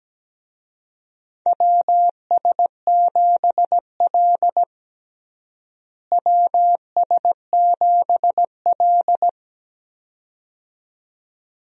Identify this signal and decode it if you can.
Morse code, heard as a keyed tone, about 17 wpm, 700 Hz. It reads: WS7L WS7L
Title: USA Call Signs - 4 Characters - Repeated 2x - 17wpm